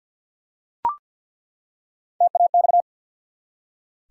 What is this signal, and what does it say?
Morse code, heard as a keyed tone, about 47 wpm, 700 Hz. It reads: TAX
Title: Three Letter Words - 50wpm